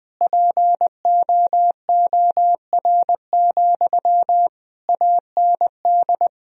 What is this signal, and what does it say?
Morse code, heard as a keyed tone, about 21 wpm, 700 Hz. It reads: POOR, AND
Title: Sentences from Top 1000 Words - Review Part 4 - 20wpm